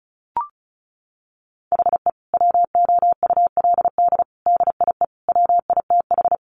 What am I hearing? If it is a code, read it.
Morse code, heard as a keyed tone, about 34 wpm, 700 Hz. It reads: HE WOULD DIE WITH
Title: Sentences from Top 1300 Words 35wpm